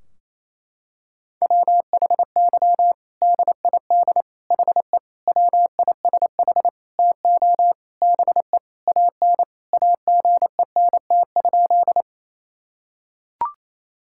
Morse code, heard as a keyed tone, about 27 wpm, 700 Hz. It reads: WHY DID HE WISH TO BE AN AGENT?
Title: Sentences from Top 1600 Words 28wpm